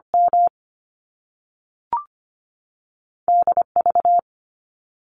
Morse code, heard as a keyed tone, about 25 wpm, 700 Hz. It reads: M D V
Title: Top Two Letter Combinations in Words 25wpm